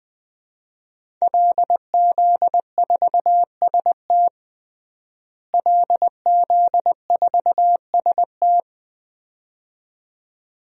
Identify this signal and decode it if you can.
Morse code, heard as a keyed tone, about 20 wpm, 700 Hz. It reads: LZ4ST LZ4ST
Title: DXCC-100 Call Signs - 5 Characters - Repeated 2x - 20wpm